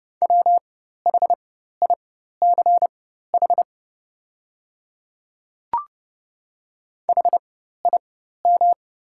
Morse code, heard as a keyed tone, about 30 wpm, spaced as for 14 wpm, 690 Hz. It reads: WHICH HIM
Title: 4x Character Spacing - Top 100 Words - 30wpm